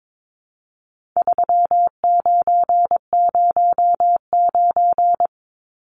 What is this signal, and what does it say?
Morse code, heard as a keyed tone, about 22 wpm, 700 Hz. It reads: 3909
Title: Numbers - 4-Digit 22wpm